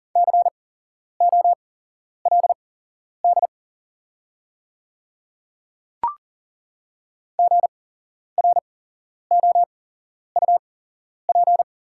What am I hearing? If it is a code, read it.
Morse code, heard as a keyed tone, about 40 wpm, spaced as for 11 wpm, 700 Hz. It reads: COLD GROUP
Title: 8x Character Spacing - Top 500 Words - 40wpm